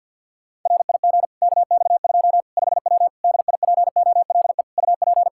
Morse code, heard as a keyed tone, about 51 wpm, 700 Hz. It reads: RIG KX2 5W DIPOLE UP
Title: Practice QSO - Exchange 2 - Info - 50wpm